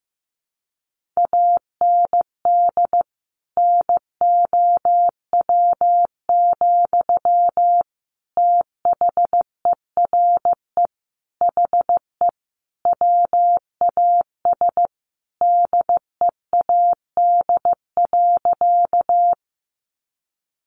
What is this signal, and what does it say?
Morse code, heard as a keyed tone, about 15 wpm, 700 Hz. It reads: AND NOW, THERE HE WAS DEAD.
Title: Sentences from Top 1500 Words 15wpm